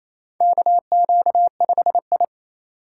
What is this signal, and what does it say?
Morse code, heard as a keyed tone, about 28 wpm, 700 Hz. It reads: KQ5I
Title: USA Call Signs - 4 Characters - 3x Character Spacing - 28wpm